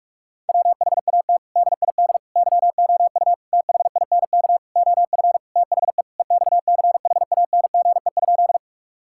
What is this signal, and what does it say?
Morse code, heard as a keyed tone, about 41 wpm, 700 Hz. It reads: WHAT DID YOU THINK OF THE EXCHANGE?
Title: Sentences from Top 1000 Words - Review Part 6 - 45wpm